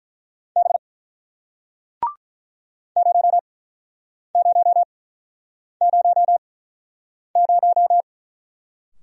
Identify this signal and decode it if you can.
Morse code, heard as a keyed tone, about 50 wpm, spaced as for 9 wpm, 700 Hz. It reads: B 0000